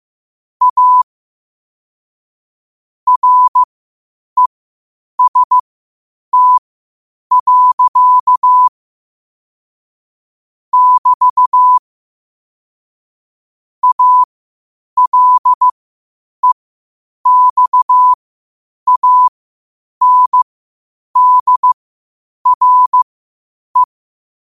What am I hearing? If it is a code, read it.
Morse code, heard as a keyed tone, about 15 wpm, spaced as for 8 wpm, 1000 Hz. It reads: A REST. <BT> ALEXANDRE
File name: qotdfiles / 8wpm